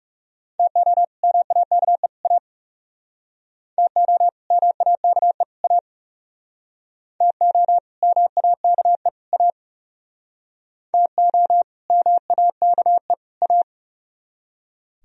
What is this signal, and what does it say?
Morse code, heard as a keyed tone, about 39 wpm, 700 Hz. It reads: TO MAKE A TO MAKE A TO MAKE A TO MAKE A